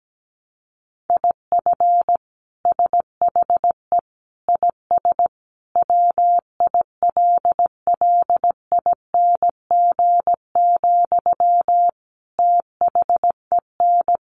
Morse code, heard as a keyed tone, about 17 wpm, 700 Hz. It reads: IF SHE IS WILLING, THEN
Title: Sentences from Top 1000 Words 17wpm